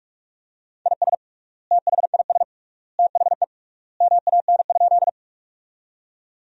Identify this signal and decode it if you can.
Morse code, heard as a keyed tone, about 41 wpm, 700 Hz. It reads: IS THIS THE MAN?
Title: 3x Word-Spacing - Sentences from Top 200 Words - 45wpm